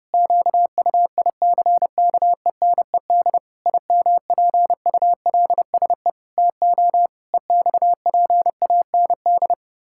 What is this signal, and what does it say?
Morse code, heard as a keyed tone, about 32 wpm, 700 Hz. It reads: QUICKENED IMPULSE TO EXPAND
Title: Pangram Challenge v2 - 30wpm